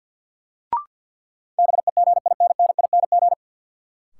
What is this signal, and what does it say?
Morse code, heard as a keyed tone, about 53 wpm, 700 Hz. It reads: BEGINNING
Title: Sentences from Top 800 Words - New Word Drill - 50wpm